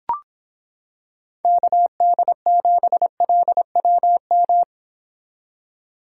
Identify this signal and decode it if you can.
Morse code, heard as a keyed tone, about 26 wpm, 700 Hz. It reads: KD7LWM